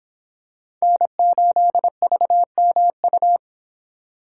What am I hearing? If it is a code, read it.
Morse code, heard as a keyed tone, about 24 wpm, 700 Hz. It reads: N8VMU